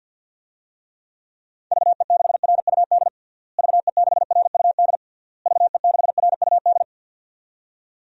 Morse code, heard as a keyed tone, about 52 wpm, 700 Hz. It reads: VE6RUD VE6RUD VE6RUD
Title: DXCC-100 Call Signs - 6 Characters - Repeated 3x - 50wpm